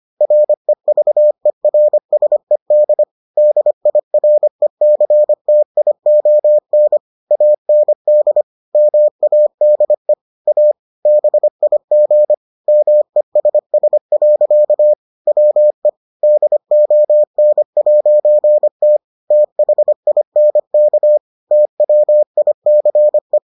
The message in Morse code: REVERSED DIRECTION AND MADE A BIG MESS. WE DON'T THINK TWICE

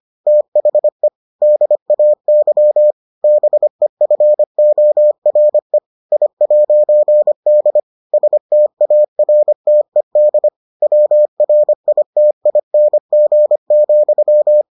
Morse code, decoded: THE DAY BEFORE I'D STARTED WRITING,

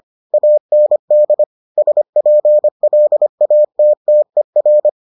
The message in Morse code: AND SPLATTER

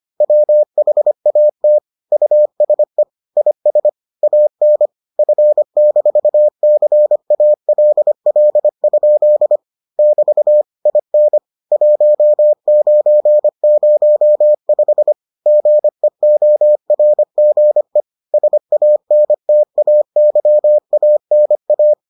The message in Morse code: WHAT USE IS AN F-CALL? <BT> IN 1905 GEORGE SANTAYANA

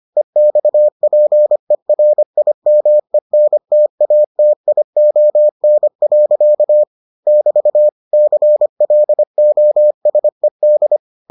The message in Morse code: EXPERIMENTATION. <BT> CLOSED

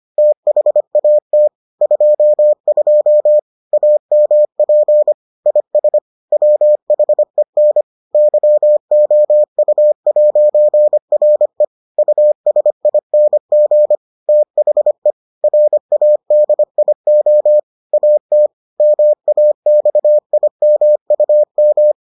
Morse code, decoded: THAT 22 AMP IS WHEN YOU'RE USING THE RADIO AT MAXIMUM

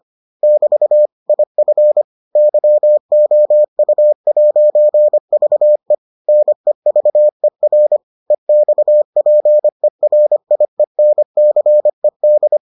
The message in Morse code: <BT> IF YOU'VE NEVER EXPERIENCED

